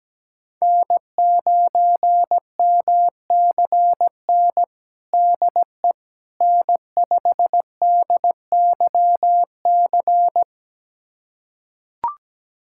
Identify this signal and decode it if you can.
Morse code, heard as a keyed tone, about 17 wpm, 700 Hz. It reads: N9MCN DE N5DYC
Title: USA Callsign DE Callsign - 5 Characters - 17wpm